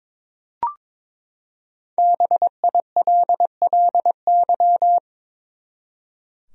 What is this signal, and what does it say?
Morse code, heard as a keyed tone, about 22 wpm, 700 Hz. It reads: BILLY
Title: USA Common Names - Max 5 Letters - 22wpm